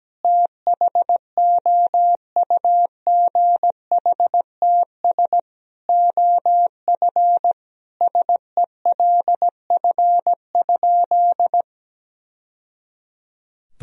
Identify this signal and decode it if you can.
Morse code, heard as a keyed tone, about 17 wpm, 700 Hz. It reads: THOUGHTS OF SELF?
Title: Sentences from Top 1000 Words - Review Part 1 - 17wpm